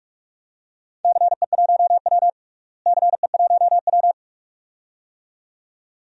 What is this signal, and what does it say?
Morse code, heard as a keyed tone, about 42 wpm, 700 Hz. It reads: CE1W CE1W
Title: DXCC-100 Call Signs - 4 Characters - Repeated 2x - 45wpm